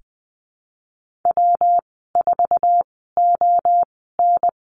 Morse code, heard as a keyed tone, about 20 wpm, spaced as for 15 wpm, 700 Hz. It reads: W4ON
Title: 161 Lesson 26g - Letter g - Call Signs 20wpm